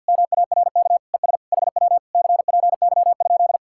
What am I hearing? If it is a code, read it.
Morse code, heard as a keyed tone, about 66 wpm, spaced as for 58 wpm, 700 Hz. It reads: MARK ES HW CPY?
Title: Speed-Racing Common QSO Phrases Encore - 50wpm